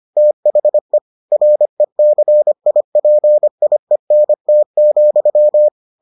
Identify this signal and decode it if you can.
Morse code, heard as a keyed tone, about 25 wpm, 600 Hz. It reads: THE RECIPIENT,